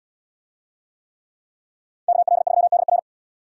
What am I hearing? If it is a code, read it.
Morse code, heard as a keyed tone, about 73 wpm, 700 Hz. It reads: KL3DR